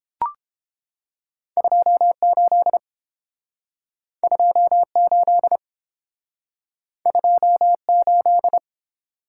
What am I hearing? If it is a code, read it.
Morse code, heard as a keyed tone, about 32 wpm, 700 Hz. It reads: 28 28 28